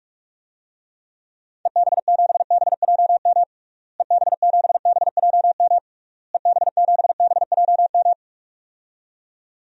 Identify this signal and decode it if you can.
Morse code, heard as a keyed tone, about 46 wpm, 700 Hz. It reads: EB7BJM EB7BJM EB7BJM
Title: DXCC-100 Call Signs - 6 Characters - Repeated 3x - 45wpm